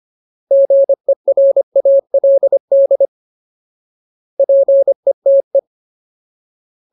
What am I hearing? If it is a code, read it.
Morse code, heard as a keyed tone, about 25 wpm, 550 Hz. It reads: GERALD PETE
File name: Ultimate Word List / 25WPM Ultimate